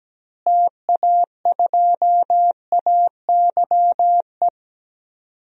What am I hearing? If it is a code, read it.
Morse code, heard as a keyed tone, about 17 wpm, 700 Hz. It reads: TA2AYE
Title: DXCC-100 Call Signs - 6 Characters - Repeated 2x - 17wpm